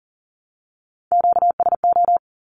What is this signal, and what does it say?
Morse code, heard as a keyed tone, about 40 wpm, 700 Hz. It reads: QSO
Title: Q-Codes Repeated 3x 40wpm